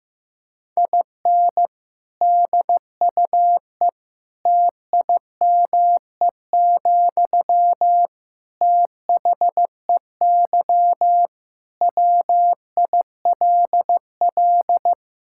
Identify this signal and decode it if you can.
Morse code, heard as a keyed tone, about 15 wpm, 700 Hz. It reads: IN DUE TIME, THEY WILL
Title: Sentences from Top 1000 Words - Review Part 6 - 15wpm